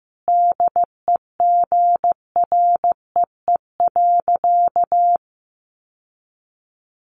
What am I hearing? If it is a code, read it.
Morse code, heard as a keyed tone, about 15 wpm, 700 Hz. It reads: DEGREE.
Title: Sentences from Top 1500 Words 15wpm